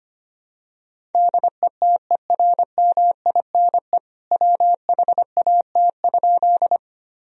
Morse code, heard as a keyed tone, about 25 wpm, 700 Hz. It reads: DETERMINE WHAT?
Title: Sentences from Top 1000 Words - Review Part 4 - 25wpm